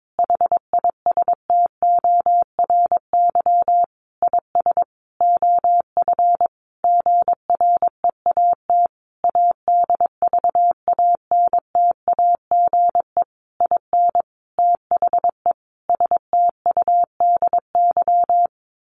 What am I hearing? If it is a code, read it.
Morse code, heard as a keyed tone, about 22 wpm, 700 Hz. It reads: HISTORY IS OF GREAT ADVANTAGE IN THE STUDY
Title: Sentences from Top 900 Words 22wpm